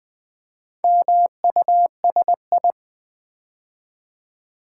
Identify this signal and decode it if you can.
Morse code, heard as a keyed tone, about 20 wpm, 700 Hz. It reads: MUSI
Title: Build A Word - 2 Letters At A Time - 8-Letter Words - 20wpm